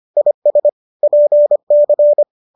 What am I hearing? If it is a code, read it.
Morse code, heard as a keyed tone, about 25 wpm, 590 Hz. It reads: IS PC
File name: qso206 25